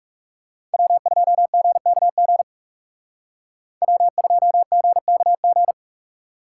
Morse code, heard as a keyed tone, about 39 wpm, 700 Hz. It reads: W2GKG W2GKG